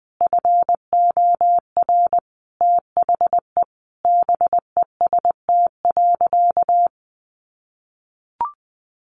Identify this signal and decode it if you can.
Morse code, heard as a keyed tone, about 20 wpm, 700 Hz. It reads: FOR THE BEST.
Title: Sentences from Top 500 Words 20wpm